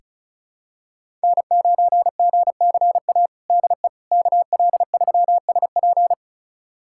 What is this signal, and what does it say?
Morse code, heard as a keyed tone, about 34 wpm, 700 Hz. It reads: N9GCA DE KL3SP
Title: USA Callsign DE Callsign - 5 Characters - 35wpm